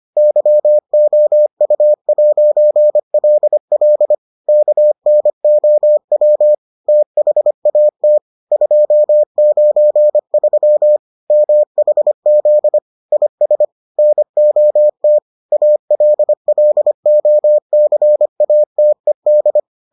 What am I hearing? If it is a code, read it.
Morse code, heard as a keyed tone, about 25 wpm, 600 Hz. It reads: YOU'LL KNOW THAT 293 MHZ IS NOT ALLOCATED